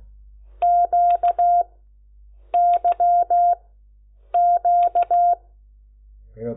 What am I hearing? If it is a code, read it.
Morse code, heard as a keyed tone, about 16 wpm, 700 Hz. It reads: Q Y Q
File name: Lecciones